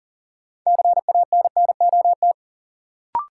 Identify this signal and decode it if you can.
Morse code, heard as a keyed tone, about 40 wpm, 700 Hz. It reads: CANNOT
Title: Sentences from Top 1000 Words - Review Part 6 - New Word Drill - 40wpm